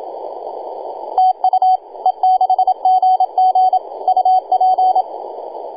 Morse code, heard as a keyed tone, about 27 wpm, 750 Hz, with radio noise. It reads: TU E6GG UP
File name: e6gg-18-083-cw